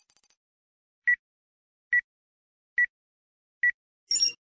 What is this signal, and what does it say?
There is no code.